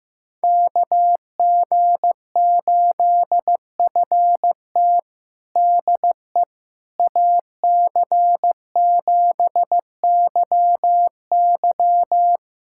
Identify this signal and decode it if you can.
Morse code, heard as a keyed tone, about 15 wpm, 700 Hz. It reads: KG8FT DE AC7YY
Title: USA Callsign DE Callsign - 5 Characters - 15wpm